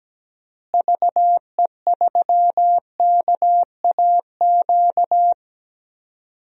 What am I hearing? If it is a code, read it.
Morse code, heard as a keyed tone, about 17 wpm, 700 Hz. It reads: VE3KAQ
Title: DXCC-100 Call Signs - 6 Characters - Repeated 2x - 17wpm